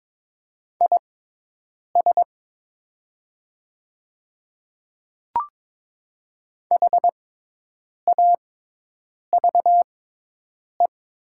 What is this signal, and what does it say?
Morse code, heard as a keyed tone, about 22 wpm, spaced as for 8 wpm, 700 Hz. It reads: IS HAVE
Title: 6x Character Spacing - Top 100 Words - 22wpm